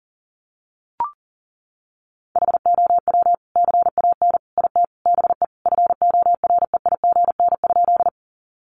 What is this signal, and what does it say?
Morse code, heard as a keyed tone, about 40 wpm, 700 Hz. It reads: HOW CAN IT BE FOREIGN?
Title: Sentences from Top 1400 Words 40wpm